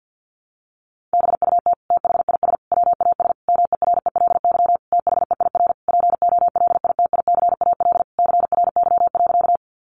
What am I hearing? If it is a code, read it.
Morse code, heard as a keyed tone, about 50 wpm, 700 Hz. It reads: BUT THIS WAS MERELY THEIR POLITICAL CRY.
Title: Sentences from Top 1400 Words 50wpm